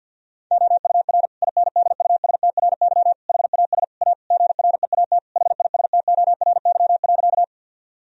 Morse code, heard as a keyed tone, about 50 wpm, 700 Hz. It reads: OUR INDUSTRY HAS A GREAT HISTORY.
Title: Sentences from Top 1000 Words - Review Part 4 - 50wpm